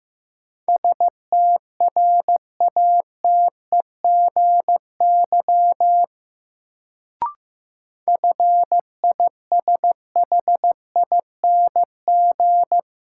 Morse code, heard as a keyed tone, about 15 wpm, 700 Hz. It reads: STRATEGY FISHING
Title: Sentences from Top 1000 Words - Review Part 5 - New Word Drill - 15wpm